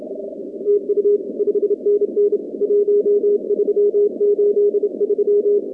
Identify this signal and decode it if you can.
Morse code, heard as a keyed tone, about 28 wpm, 410 Hz. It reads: TU 5NN 1383
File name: contests / 2019-cw-wpx-kt5j-k5pi- -2330 0-44